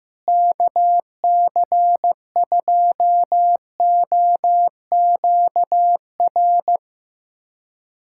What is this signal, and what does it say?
Morse code, heard as a keyed tone, about 15 wpm, 700 Hz. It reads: KC2OQR
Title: USA Callsign DE Callsign 15wpm